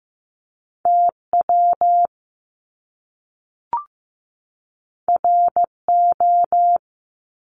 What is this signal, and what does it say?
Morse code, heard as a keyed tone, about 15 wpm, 700 Hz. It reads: TW RO
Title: Top Two Letter Combinations in Words 15wpm